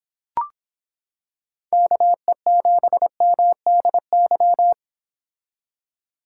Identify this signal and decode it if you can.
Morse code, heard as a keyed tone, about 26 wpm, 700 Hz. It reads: KE7MDY